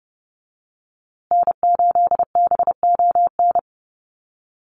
Morse code, heard as a keyed tone, about 30 wpm, 700 Hz. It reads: N8BON
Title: USA Call Signs Repeated 2x 30wpm